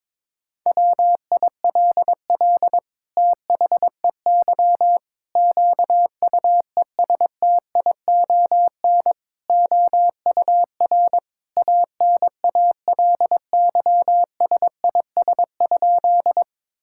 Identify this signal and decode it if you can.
Morse code, heard as a keyed tone, about 21 wpm, 700 Hz. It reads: WILL THEY QUESTION OUR ANALYSIS?
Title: Sentences from Top 1000 Words - Review Part 5 - 22wpm